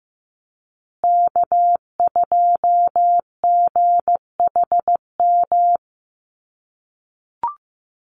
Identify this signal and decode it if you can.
Morse code, heard as a keyed tone, about 15 wpm, 700 Hz. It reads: K2GHM
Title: USA Call Signs Repeated 3x 15wpm